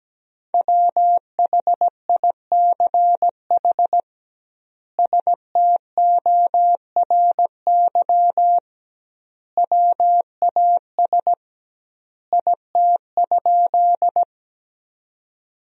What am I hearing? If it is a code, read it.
Morse code, heard as a keyed tone, about 17 wpm, 700 Hz. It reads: WHICH STORY WAS IT?
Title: 2x Word-Spacing - Sentences from Top 300 Words - 17wpm